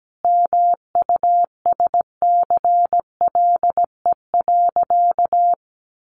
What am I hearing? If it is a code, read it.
Morse code, heard as a keyed tone, about 17 wpm, 700 Hz. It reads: MUSCLE.
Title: Sentences from Top 1300 Words 17wpm